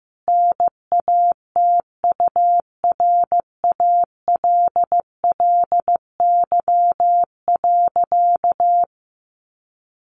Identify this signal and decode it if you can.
Morse code, heard as a keyed tone, about 15 wpm, 700 Hz. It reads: NATURALLY.
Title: Sentences from Top 1300 Words 15wpm